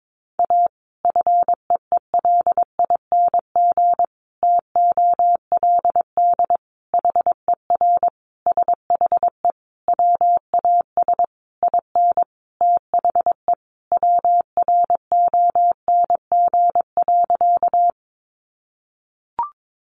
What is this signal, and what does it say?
Morse code, heard as a keyed tone, about 22 wpm, 700 Hz. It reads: A FEELING TOLD HER SHE WAS IN THE WRONG.
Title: Sentences from Top 1200 Words 22wpm